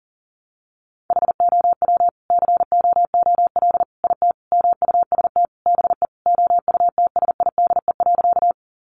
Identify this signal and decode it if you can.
Morse code, heard as a keyed tone, about 40 wpm, 700 Hz. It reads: HOW COOL IT MUST BE OUTSIDE.
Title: Sentences from Top 1200 Words 40wpm